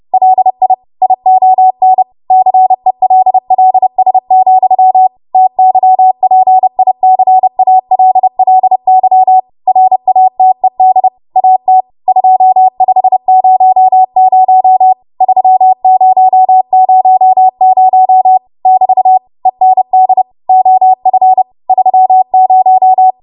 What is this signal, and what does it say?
Morse code, heard as a keyed tone, about 30 wpm, 750 Hz. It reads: LI ION CELLS, TYPICALLY RATED AT 2500 3000 <BT> END OF 30